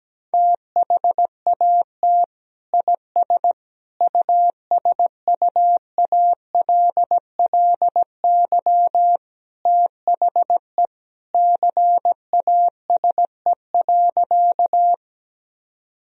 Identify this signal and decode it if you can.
Morse code, heard as a keyed tone, about 17 wpm, 700 Hz. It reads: THAT IS USUALLY THE CASE.
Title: Sentences from Top 1000 Words - Review Part 3 - 17wpm